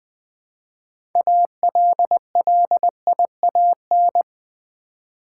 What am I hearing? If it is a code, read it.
Morse code, heard as a keyed tone, about 20 wpm, 700 Hz. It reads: ALLIAN
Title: Build A Word - 2 Letters At A Time - 8-Letter Words - 20wpm